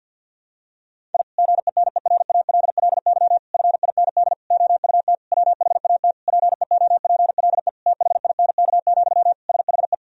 Morse code, heard as a keyed tone, about 50 wpm, 700 Hz. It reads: I GENERALLY FIND OUT WHAT PEOPLE THINK, SHE
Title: Sentences from Top 1000 Words - Review Part 4 - 50wpm